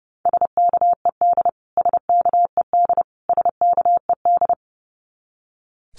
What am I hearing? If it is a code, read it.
Morse code, heard as a keyed tone, about 30 wpm, 700 Hz. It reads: SKED SKED SKED
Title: Abbreviations Repeated 3x - 30wpm